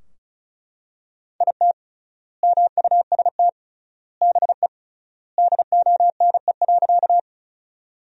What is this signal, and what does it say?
Morse code, heard as a keyed tone, about 34 wpm, 700 Hz. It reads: IT MUST BE DONE.
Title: 3x Word-Spacing - Sentences from Top 500 Words - 35wpm